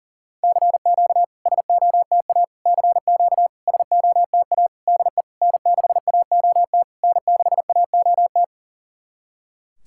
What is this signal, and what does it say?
Morse code, heard as a keyed tone, about 40 wpm, 700 Hz. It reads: CQ SOTA CQ SOTA DE N6AOT N6AOT
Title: SOTA QSO Practice 40wpm